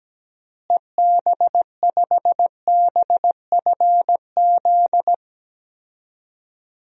Morse code, heard as a keyed tone, about 17 wpm, 700 Hz. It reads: EB5BFZ
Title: DXCC-100 Call Signs - 6 Characters - Repeated 2x - 17wpm